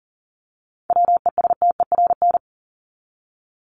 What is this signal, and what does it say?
Morse code, heard as a keyed tone, about 40 wpm, 700 Hz. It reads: WESTERN
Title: Top 751-1000 Words 40wpm